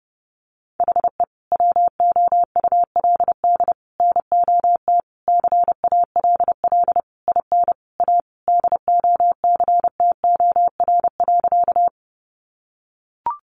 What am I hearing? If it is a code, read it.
Morse code, heard as a keyed tone, about 30 wpm, 700 Hz. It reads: HE WOULD NOT CALL IN A DOCTOR.
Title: Sentences from Top 1500 Words 30wpm